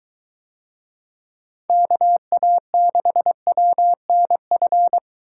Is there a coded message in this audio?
Yes: Morse code from 1.7 to 5.0 s, heard as a keyed tone, about 22 wpm, 700 Hz. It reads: KA6WNF